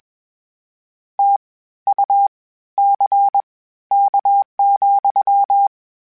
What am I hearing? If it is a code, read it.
Morse code, heard as a keyed tone, about 21 wpm, spaced as for 13 wpm, 800 Hz. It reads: TUCK,
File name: Misc / Misc-mp3s / WordPairs sp @20